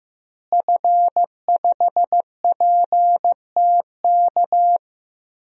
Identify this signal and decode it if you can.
Morse code, heard as a keyed tone, about 15 wpm, 700 Hz. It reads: F5PTK
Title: DXCC-100 Call Signs - 5 Characters - Repeated 2x - 15wpm